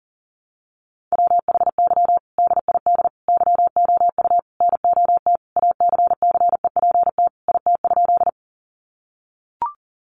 Morse code, heard as a keyed tone, about 40 wpm, 700 Hz. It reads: WHY DID YOU NOT ACCEPT IT?
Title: Sentences from Top 1300 Words 40wpm